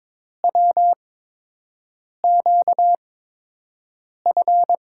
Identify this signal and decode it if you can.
Morse code, heard as a keyed tone, about 22 wpm, spaced as for 6 wpm, 700 Hz. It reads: WQF
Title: USA Call Signs - 6 Characters - 8x Character Spacing - 22wpm